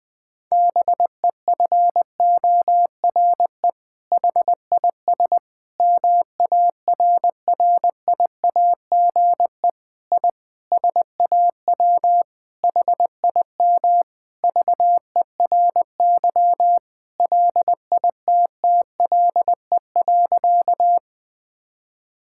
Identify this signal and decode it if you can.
Morse code, heard as a keyed tone, about 20 wpm, 700 Hz. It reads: BEFORE HIS MARRIAGE I SAW HIM VERY LITTLE.
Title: Sentences from Top 1000 Words - Review Part 5 - 20wpm